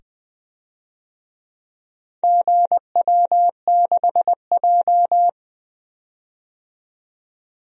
Morse code, heard as a keyed tone, about 21 wpm, 700 Hz. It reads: GW6J
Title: DXCC-100 Call Signs - 4 Characters - 20wpm